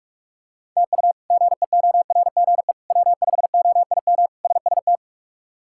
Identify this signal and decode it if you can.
Morse code, heard as a keyed tone, about 42 wpm, 700 Hz. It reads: TU GEORGE W5OIM SST
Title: K1USN SST QSO Practice - 45wpm